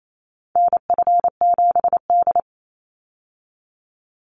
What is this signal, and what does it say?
Morse code, heard as a keyed tone, about 28 wpm, 700 Hz. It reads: NF7D
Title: Call Signs - 28wpm